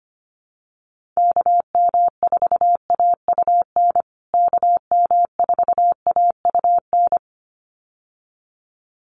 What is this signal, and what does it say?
Morse code, heard as a keyed tone, about 25 wpm, 700 Hz. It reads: KM4AUN KM4AUN
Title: USA Call Signs Repeated 2x 25wpm